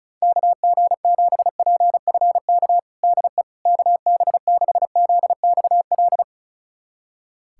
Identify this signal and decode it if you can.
Morse code, heard as a keyed tone, about 36 wpm, 700 Hz. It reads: KG7PFK DE KB6ZXL
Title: USA Callsign DE Callsign - 6 Characters - 35wpm